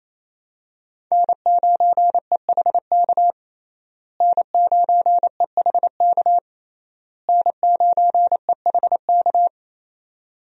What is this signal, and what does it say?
Morse code, heard as a keyed tone, about 28 wpm, 700 Hz. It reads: N9EHK N9EHK N9EHK
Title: USA Call Signs - 5 Characters - Repeated 3x - 28wpm